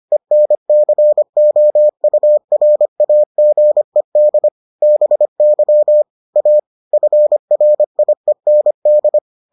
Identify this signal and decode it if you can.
Morse code, heard as a keyed tone, about 25 wpm, 600 Hz. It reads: ENCOURAGED BY A FRIEND